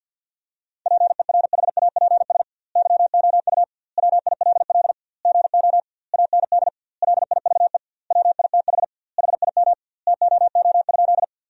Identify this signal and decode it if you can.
Morse code, heard as a keyed tone, about 51 wpm, 700 Hz. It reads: PERHAPS YOU WILL GO AND LIVE WITH HIM TOO?
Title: 2x Word-Spacing - Sentences from Top 500 Words - 50wpm